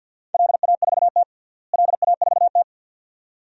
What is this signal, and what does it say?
Morse code, heard as a keyed tone, about 53 wpm, 700 Hz. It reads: LA4T LA4T
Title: DXCC-100 Call Signs - 4 Characters - Repeated 2x - 50wpm